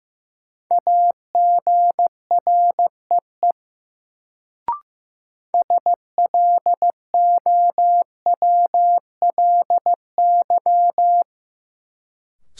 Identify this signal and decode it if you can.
Morse code, heard as a keyed tone, about 15 wpm, 700 Hz. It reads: AGREE SLOWLY
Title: Sentences from Top 1100 Words - New Word Drill Alt - 15wpm